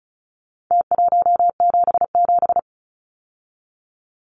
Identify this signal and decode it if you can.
Morse code, heard as a keyed tone, about 34 wpm, 700 Hz. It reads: T177
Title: Numbers - Contest Style - 4-Digit 35wpm